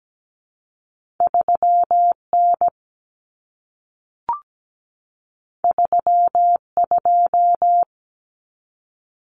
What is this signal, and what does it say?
Morse code, heard as a keyed tone, about 17 wpm, 700 Hz. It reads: 3N 32
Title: Common Cut Numbers - 2-Digit 17wpm